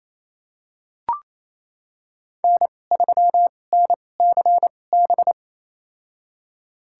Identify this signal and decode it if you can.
Morse code, heard as a keyed tone, about 28 wpm, spaced as for 20 wpm, 700 Hz. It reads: N3NCB
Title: USA Call Signs - 5 Characters - 2x Character Spacing - 28wpm